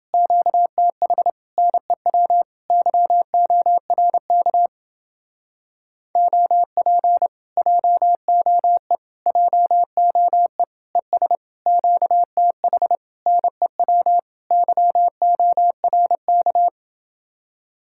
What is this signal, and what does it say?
Morse code, heard as a keyed tone, about 29 wpm, 700 Hz. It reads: QTH NEW YORK OP JOE JOE ES QTH NEW YORK